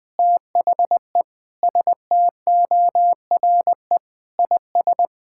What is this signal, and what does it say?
Morse code, heard as a keyed tone, about 20 wpm, 700 Hz. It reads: THE STORE IS
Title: Sentences with 7-Word Phrases - 20wpm